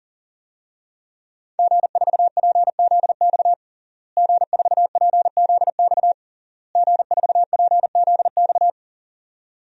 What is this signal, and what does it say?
Morse code, heard as a keyed tone, about 37 wpm, 700 Hz. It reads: G4PZX G4PZX G4PZX
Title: DXCC-100 Call Signs - 5 Characters - Repeated 3x - 40wpm